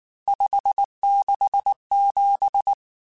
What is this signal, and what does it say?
Morse code, heard as a keyed tone, about 19 wpm, 800 Hz. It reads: 567
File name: RSTs / RST-mp3s / Various RST @18